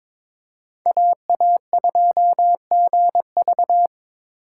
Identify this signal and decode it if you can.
Morse code, heard as a keyed tone, about 22 wpm, 700 Hz. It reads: AA2GV
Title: USA Call Signs - 5 Characters - 3x Character Spacing - 22wpm